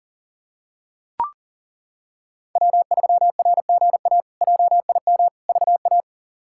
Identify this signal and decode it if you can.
Morse code, heard as a keyed tone, about 37 wpm, 700 Hz. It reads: W3RGA JIM VA
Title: K1USN SST QSO Practice - 40wpm